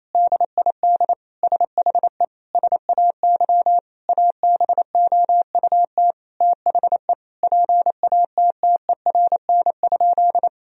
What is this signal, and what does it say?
Morse code, heard as a keyed tone, about 28 wpm, 700 Hz. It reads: DID SHE SAY ABOUT THE PATTERN?
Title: Sentences from Top 1000 Words - Review Part 2 - 28wpm